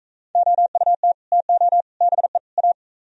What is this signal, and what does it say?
Morse code, heard as a keyed tone, about 45 wpm, 700 Hz. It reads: OUT TO BE A